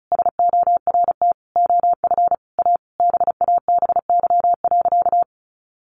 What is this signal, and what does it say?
Morse code, heard as a keyed tone, about 34 wpm, 700 Hz. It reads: SORT OF A BABY.
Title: Sentences from Top 1400 Words 35wpm